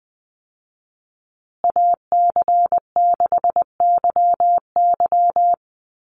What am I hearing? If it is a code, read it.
Morse code, heard as a keyed tone, about 20 wpm, 700 Hz. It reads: AC6YY
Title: Call Signs Repeated 2x - 20wpm